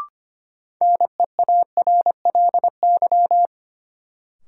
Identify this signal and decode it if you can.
Morse code, heard as a keyed tone, about 25 wpm, 700 Hz. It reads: NEARLY
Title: Sentences from Top 1300 Words - New Word Drill - 25wpm